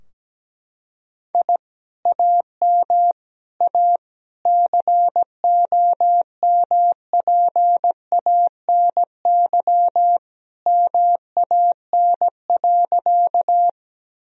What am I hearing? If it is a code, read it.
Morse code, heard as a keyed tone, about 17 wpm, 700 Hz. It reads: I AM A COMPANY MAN.
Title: Sentences from Top 1000 Words - Review Part 3 - 17wpm